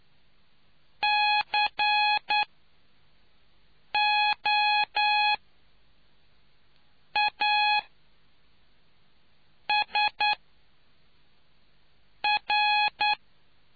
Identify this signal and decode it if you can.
Morse code, heard as a keyed tone, about 10 wpm, spaced as for 4 wpm, 840 Hz. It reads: COASR